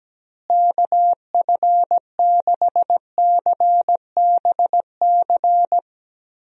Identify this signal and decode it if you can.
Morse code, heard as a keyed tone, about 17 wpm, 700 Hz. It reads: KF6CBC
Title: USA Call Signs - 6 Characters - Repeated 2x - 17wpm